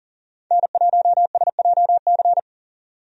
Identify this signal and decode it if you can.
Morse code, heard as a keyed tone, about 40 wpm, 700 Hz. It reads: N1SJC
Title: USA Call Signs - 5 Characters - 40wpm